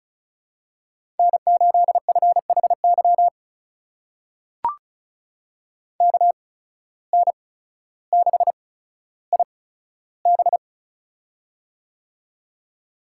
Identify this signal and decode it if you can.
Morse code, heard as a keyed tone, about 34 wpm, spaced as for 15 wpm, 700 Hz. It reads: N8FHY KN6IB
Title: USA Call Signs - 8x Character Spacing - 35wpm